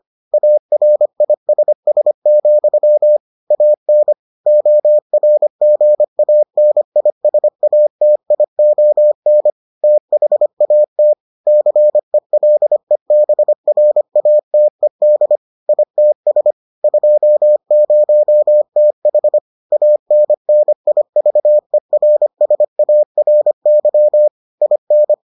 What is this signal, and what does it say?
Morse code, heard as a keyed tone, about 25 wpm, 600 Hz. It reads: ARISS, AN ORGANISATION THAT CELEBRATED ITS 20TH ANNIVERSARY IN